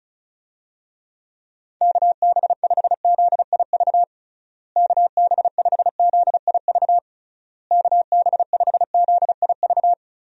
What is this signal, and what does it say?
Morse code, heard as a keyed tone, about 36 wpm, 700 Hz. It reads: KB5ZIV KB5ZIV KB5ZIV
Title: USA Call Signs - 6 Characters - Repeated 3x - 35wpm